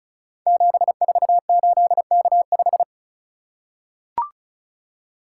Morse code, heard as a keyed tone, about 36 wpm, 690 Hz. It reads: 748K5
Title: Code Groups - 5 Letter:Numbers - 35wpm